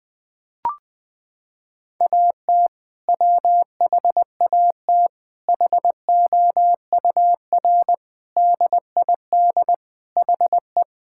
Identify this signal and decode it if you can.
Morse code, heard as a keyed tone, about 20 wpm, 700 Hz. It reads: AT WHAT HOUR DID HE
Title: Sentences from Top 1600 Words 20wpm